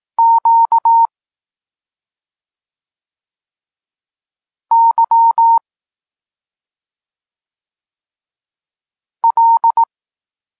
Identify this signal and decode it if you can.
Morse code, heard as a keyed tone, about 18 wpm, 920 Hz. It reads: Q Y L